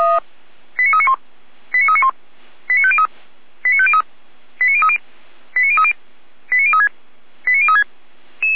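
Five-tone selective call (ZVEI), heard as quick runs of five tones, tones of about 70 ms. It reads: 89391 89391 89503 89503 89030 89030 89036 89036
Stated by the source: Five-tone ZVEI selective call received on an SDR.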